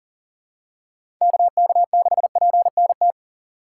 Morse code, heard as a keyed tone, about 40 wpm, 700 Hz. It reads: KK6PNT
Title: USA Call Signs - 6 Characters - 2x Character Spacing - 40wpm